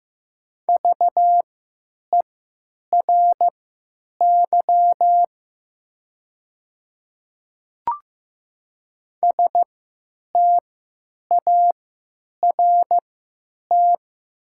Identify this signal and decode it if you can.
Morse code, heard as a keyed tone, about 15 wpm, spaced as for 9 wpm, 700 Hz. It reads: VERY START
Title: 3x Character Spacing - Top 300 Words - 15wpm